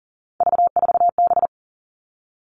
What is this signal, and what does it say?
Morse code, heard as a keyed tone, about 40 wpm, 700 Hz. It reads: V4B
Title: All Cut Numbers - 3-Digit 40wpm